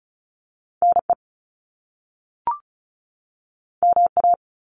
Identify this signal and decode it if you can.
Morse code, heard as a keyed tone, about 36 wpm, 700 Hz. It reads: NE MA
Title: US States 35wpm